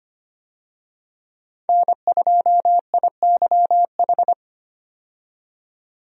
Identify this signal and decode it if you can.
Morse code, heard as a keyed tone, about 25 wpm, 700 Hz. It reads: N2IYH
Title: USA Call Signs - 5 Characters - 25wpm